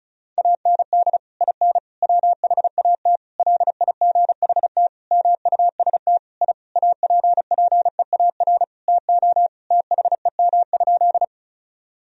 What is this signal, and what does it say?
Morse code, heard as a keyed tone, about 36 wpm, 700 Hz. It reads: AND IN WHAT LIGHT MUST I APPEAR TO THEM?
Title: Sentences from Top 1000 Words - Review Part 2 - 35wpm